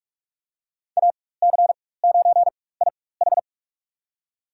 Morse code, heard as a keyed tone, about 44 wpm, spaced as for 21 wpm, 700 Hz. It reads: AC9IH
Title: USA Call Signs - 5 Characters - 4x Character Spacing - 45wpm